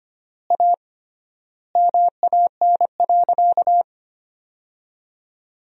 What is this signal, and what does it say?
Morse code, heard as a keyed tone, about 25 wpm, 700 Hz. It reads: A MAN.
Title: 3x Word-Spacing - Sentences from Top 300 Words - 25wpm